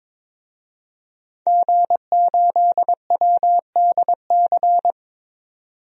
Morse code, heard as a keyed tone, about 22 wpm, 700 Hz. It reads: G8WDC
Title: DXCC-100 Call Signs - 5 Characters - 22wpm